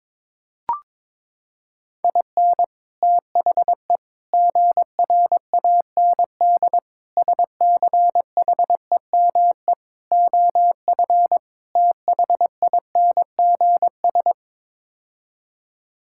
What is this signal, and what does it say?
Morse code, heard as a keyed tone, about 22 wpm, 700 Hz. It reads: IN THE GRAND SCHEME OF THINGS
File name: practice / 22wpm / Sets of 6 Words - 22wpm